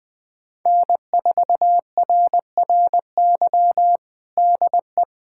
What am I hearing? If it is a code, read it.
Morse code, heard as a keyed tone, about 20 wpm, 700 Hz. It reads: N4RRY DE
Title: USA Callsign DE Callsign - 5 Characters - 20wpm